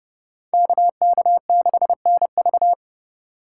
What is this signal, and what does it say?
Morse code, heard as a keyed tone, about 29 wpm, 700 Hz. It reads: KK6NV